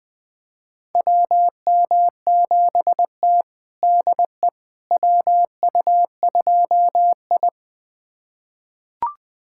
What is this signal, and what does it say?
Morse code, heard as a keyed tone, about 20 wpm, 700 Hz. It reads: WM7T DE WU2I
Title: USA Callsign DE Callsign - 4 Characters - 20wpm